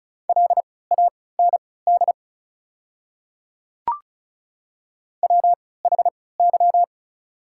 Morse code, heard as a keyed tone, about 36 wpm, spaced as for 20 wpm, 690 Hz. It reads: LAND WHY
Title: 3x Character Spacing - Top 200 Words - 35wpm